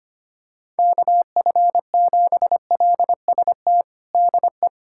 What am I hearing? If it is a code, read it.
Morse code, heard as a keyed tone, about 25 wpm, 700 Hz. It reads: KF7LST DE
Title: USA Callsign DE Callsign 25wpm